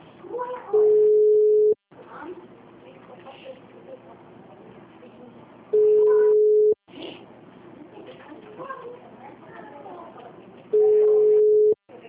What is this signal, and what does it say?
Ringback tone (Europe, 425 Hz), with speech.